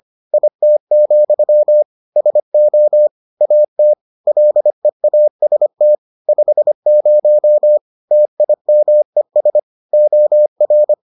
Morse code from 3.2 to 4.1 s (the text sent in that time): AT